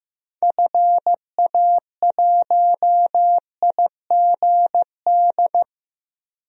Morse code, heard as a keyed tone, about 16 wpm, 700 Hz. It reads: FA1IGD
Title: DXCC-100 Call Signs - 6 Characters - 15wpm